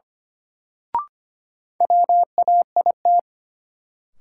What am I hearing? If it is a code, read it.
Morse code, heard as a keyed tone, about 25 wpm, 700 Hz. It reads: WAIT
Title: Sentences from Top 500 Words - New Word Drill - 25wpm